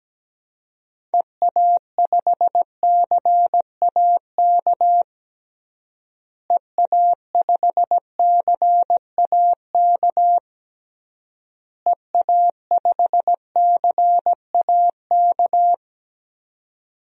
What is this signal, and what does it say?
Morse code, heard as a keyed tone, about 17 wpm, 700 Hz. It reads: EA5CAK EA5CAK EA5CAK
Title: DXCC-100 Call Signs - 6 Characters - Repeated 3x - 17wpm